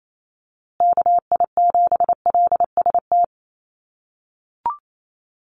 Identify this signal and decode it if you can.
Morse code, heard as a keyed tone, about 28 wpm, 700 Hz. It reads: KI7LST
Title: USA Call Signs Repeated 2x 28wpm